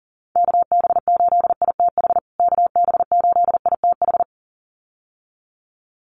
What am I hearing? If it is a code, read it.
Morse code, heard as a keyed tone, about 40 wpm, 700 Hz. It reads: KB8ITH KB8ITH
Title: USA Call Signs Repeated 2x 40wpm